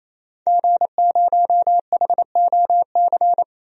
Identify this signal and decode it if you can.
Morse code, heard as a keyed tone, about 28 wpm, 700 Hz. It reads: G0HOC
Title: DXCC-100 Call Signs - 5 Characters - Repeated 3x - 28wpm